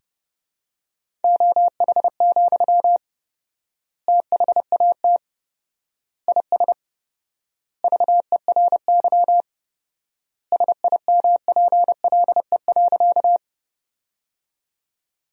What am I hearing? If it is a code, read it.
Morse code, heard as a keyed tone, about 30 wpm, 700 Hz. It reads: OH, THAT IS VERY SIMPLE.
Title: 4x Word-Spacing - Sentences from Top 500 Words - 30wpm